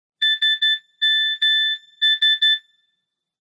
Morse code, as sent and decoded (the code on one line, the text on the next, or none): ... -- ...
SMS